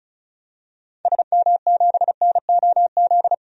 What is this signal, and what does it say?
Morse code, heard as a keyed tone, about 34 wpm, 700 Hz. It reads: SM7NOZ
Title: DXCC-100 Call Signs - 6 Characters - Repeated 3x - 35wpm